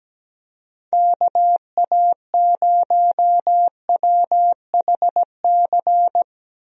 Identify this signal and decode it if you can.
Morse code, heard as a keyed tone, about 17 wpm, 700 Hz. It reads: KA0WHC
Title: USA Call Signs - 6 Characters - Repeated 2x - 17wpm